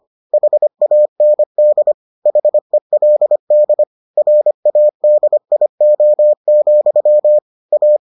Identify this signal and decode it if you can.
Morse code, heard as a keyed tone, about 25 wpm, 600 Hz. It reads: HAND HELD RADIO, A